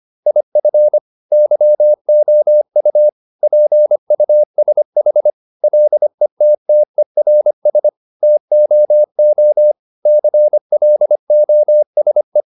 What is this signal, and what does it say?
Morse code, heard as a keyed tone, about 25 wpm, 600 Hz. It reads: IF YOU PUSH LETTERS TOO CLOSE